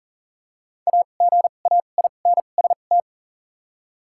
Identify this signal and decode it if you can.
Morse code, heard as a keyed tone, about 40 wpm, spaced as for 29 wpm, 700 Hz. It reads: AGAINST
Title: 2x Character Spacing - Top 500 Words - 40wpm